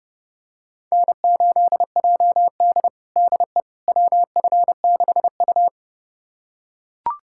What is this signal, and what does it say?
Morse code, heard as a keyed tone, about 30 wpm, 700 Hz. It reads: N8JD DE WF6U
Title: USA Callsign DE Callsign - 4 Characters - 30wpm